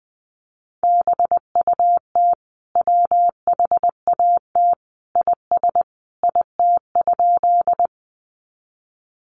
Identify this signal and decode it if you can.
Morse code, heard as a keyed tone, about 20 wpm, 700 Hz. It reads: BUT WHAT IS IT?
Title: Sentences from Top 100 Words 20wpm